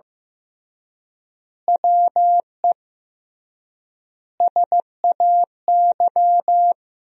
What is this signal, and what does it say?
Morse code, heard as a keyed tone, about 15 wpm, 700 Hz. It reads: WE SAY
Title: 3x Word-Spacing - Sentences from Top 200 Words - 15wpm